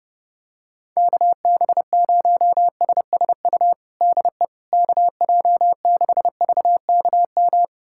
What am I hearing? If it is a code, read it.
Morse code, heard as a keyed tone, about 30 wpm, 700 Hz. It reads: KB0SSU DE KJ6VKM
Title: USA Callsign DE Callsign - 6 Characters - 30wpm